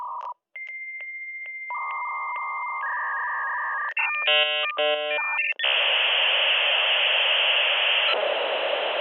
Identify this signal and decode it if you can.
Dial-up modem handshake.